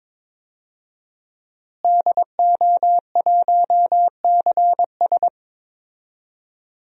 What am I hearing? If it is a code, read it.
Morse code, heard as a keyed tone, about 21 wpm, 700 Hz. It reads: DO1CS
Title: DXCC-100 Call Signs - 5 Characters - Repeated 3x - 22wpm